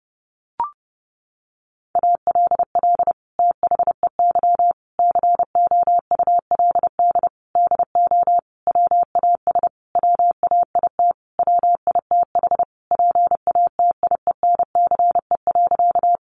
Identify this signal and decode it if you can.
Morse code, heard as a keyed tone, about 30 wpm, 700 Hz. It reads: ALL THEY COULD DO WAS WAIT WITH PATIENCE.
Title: Sentences from Top 1300 Words 30wpm